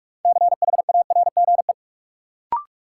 Morse code, heard as a keyed tone, about 43 wpm, 700 Hz. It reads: CHARGE
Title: Sentences from Top 900 Words - New Word Drill - 45wpm